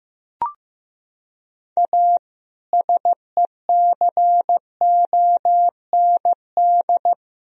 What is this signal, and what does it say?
Morse code, heard as a keyed tone, about 15 wpm, 700 Hz. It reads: A SECOND
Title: Sentences from Top 300 Words and Abbreviations 15wpm